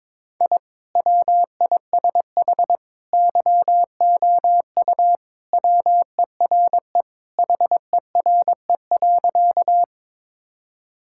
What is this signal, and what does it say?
Morse code, heard as a keyed tone, about 21 wpm, 700 Hz. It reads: I WISH YOU WERE HERE.
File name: practice / 22wpm / Sentences with 2-Word Phrases - 22wpm